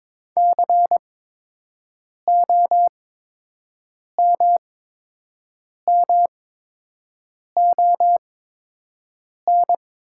Morse code, heard as a keyed tone, about 22 wpm, spaced as for 6 wpm, 700 Hz. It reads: COMMON
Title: 8x Character Spacing - Top 500 Words - 22wpm